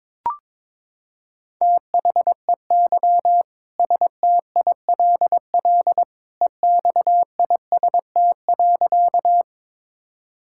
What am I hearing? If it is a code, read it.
Morse code, heard as a keyed tone, about 22 wpm, 700 Hz. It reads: THEY STILL EXIST.
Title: Sentences from Top 1000 Words - Review Part 5 - 22wpm